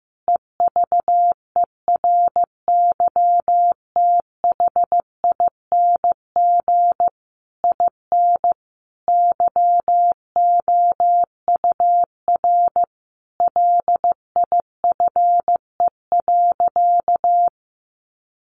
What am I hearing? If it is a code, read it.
Morse code, heard as a keyed tone, about 15 wpm, 700 Hz. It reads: EVERYTHING IN YOUR LIFE.
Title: Sentences from Top 700 Words 15wpm